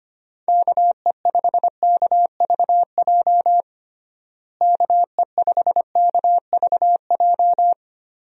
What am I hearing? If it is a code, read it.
Morse code, heard as a keyed tone, about 25 wpm, 700 Hz. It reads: KE5KVJ KE5KVJ
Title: USA Call Signs - 6 Characters - Repeated 3x - 25wpm